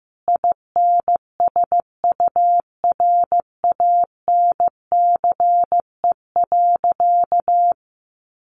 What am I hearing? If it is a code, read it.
Morse code, heard as a keyed tone, about 15 wpm, 700 Hz. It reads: INSURANCE.
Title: Sentences from Top 1100 Words 15wpm